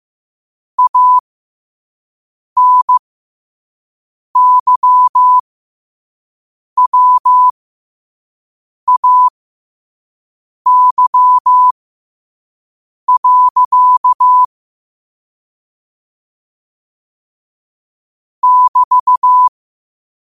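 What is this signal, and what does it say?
Morse code, heard as a keyed tone, about 15 wpm, spaced as for 5 wpm, 1000 Hz. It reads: ANYWAY. <BT>
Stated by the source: Morse code quote-of-the-day practice at 5 wpm.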